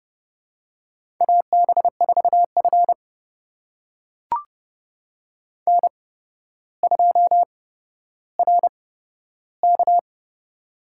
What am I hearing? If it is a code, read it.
Morse code, heard as a keyed tone, about 30 wpm, spaced as for 13 wpm, 700 Hz. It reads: AB4F N2RK
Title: USA Call Signs - 4 Characters - 8x Character Spacing - 30wpm